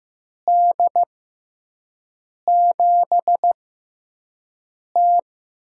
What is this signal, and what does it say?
Morse code, heard as a keyed tone, about 15 wpm, spaced as for 5 wpm, 700 Hz. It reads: D7T
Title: USA Call Signs - 6 Characters - 6x Character Spacing - 15wpm